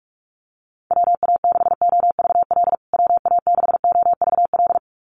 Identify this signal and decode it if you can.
Morse code, heard as a keyed tone, about 44 wpm, 700 Hz. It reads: WA6OVL WA6OVL
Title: USA Call Signs Repeated 2x 45wpm